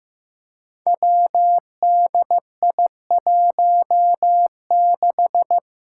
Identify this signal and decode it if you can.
Morse code, heard as a keyed tone, about 15 wpm, 700 Hz. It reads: WDI16
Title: Five Challenging Letter-Number Combinations 15wpm